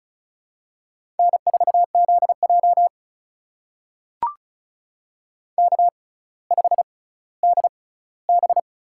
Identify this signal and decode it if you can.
Morse code, heard as a keyed tone, about 36 wpm, spaced as for 18 wpm, 700 Hz. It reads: N4ZJ K5DB
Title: USA Call Signs - 6x Character Spacing - 35wpm